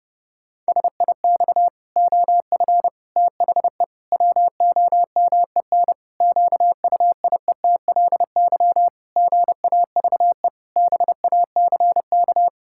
Morse code, heard as a keyed tone, about 30 wpm, 700 Hz. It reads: SIX OF THE WOMEN QUIETLY GAVE BACK
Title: Pangram Challenge v2 - 30wpm